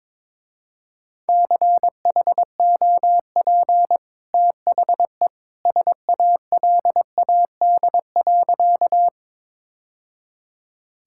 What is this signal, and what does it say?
Morse code, heard as a keyed tone, about 22 wpm, 700 Hz. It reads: CHOP THE SALAD.
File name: practice / 22wpm / Easy Sentences with Hints - 22wpm